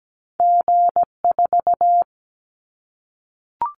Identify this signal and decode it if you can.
Morse code, heard as a keyed tone, about 17 wpm, 700 Hz. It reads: G4
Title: Callsign Prefixes for Top 25 DXCC - 17wpm